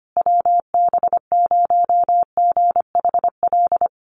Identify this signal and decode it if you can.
Morse code, heard as a keyed tone, about 25 wpm, 700 Hz. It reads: WB0GHL
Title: USA Call Signs Repeated 2x 25wpm